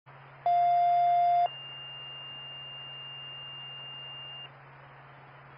Two-tone page, heard as a one-second tone, then a three-second tone, about 700 then 2800 Hz. It is heard from 0.5 to 4.5 s.